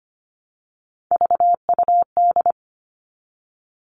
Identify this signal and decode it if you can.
Morse code, heard as a keyed tone, about 25 wpm, 700 Hz. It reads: VUD
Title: Three Challenging Letter-Number Combinations 25wpm